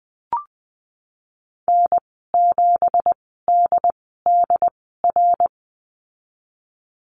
Morse code, heard as a keyed tone, about 19 wpm, spaced as for 14 wpm, 700 Hz. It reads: N7DDR
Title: 179 Lesson 29g - Slash - Call Signs 20wpm